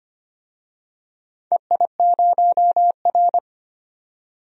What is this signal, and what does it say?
Morse code, heard as a keyed tone, about 25 wpm, 700 Hz. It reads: EI0R
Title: DXCC-100 Call Signs - 4 Characters - 25wpm